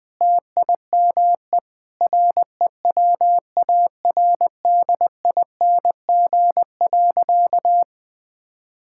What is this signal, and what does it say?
Morse code, heard as a keyed tone, about 20 wpm, 700 Hz. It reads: TIME REWARDING.
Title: Sentences with 5-Word Phrases - Spoken Beginning - 20wpm